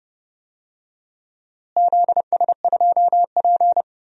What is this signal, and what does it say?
Morse code, heard as a keyed tone, about 30 wpm, 700 Hz. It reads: ZS2P
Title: DXCC-100 Call Signs - 4 Characters - Repeated 2x - 30wpm